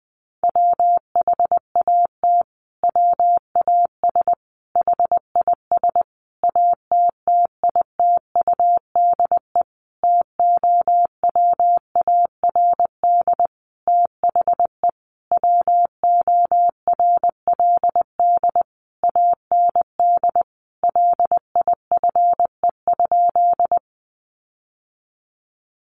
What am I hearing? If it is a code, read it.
Morse code, heard as a keyed tone, about 20 wpm, 700 Hz. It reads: WHAT WAS HIS ATTITUDE TOWARD THE WORLD AND LIFE?
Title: Sentences from Top 1300 Words 20wpm